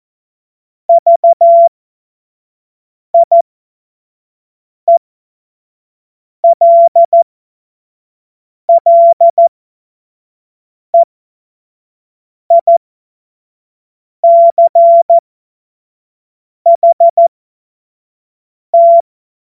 Morse code, heard as a keyed tone, about 14 wpm, spaced as for 5 wpm, 680 Hz. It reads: VIELLEICHT
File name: Morsekurs / Witze / Witz 17. Mai 2022 25 F70